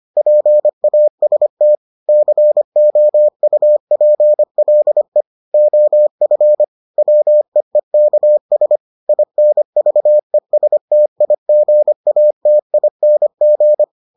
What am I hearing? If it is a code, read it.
Morse code, heard as a keyed tone, about 25 wpm, 600 Hz. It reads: PAST COUPLE OF WEEKS INVESTIGATING